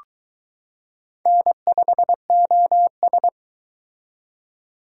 Morse code, heard as a keyed tone, about 21 wpm, 700 Hz. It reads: N5OS